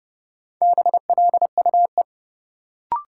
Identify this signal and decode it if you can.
Morse code, heard as a keyed tone, about 30 wpm, 700 Hz. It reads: BLUE
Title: Sentences from Top 1200 Words - New Word Drill Alt - 30wpm